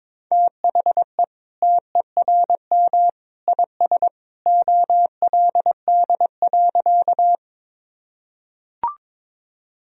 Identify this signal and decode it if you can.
Morse code, heard as a keyed tone, about 22 wpm, 700 Hz. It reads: THE TERM IS OLD.
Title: Sentences from Top 1000 Words - Review Part 4 - 22wpm